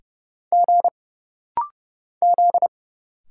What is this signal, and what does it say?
Morse code, heard as a keyed tone, about 30 wpm, 700 Hz. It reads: G Z
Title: Single Letter-Number Alt - Rapid-Fire 30wpm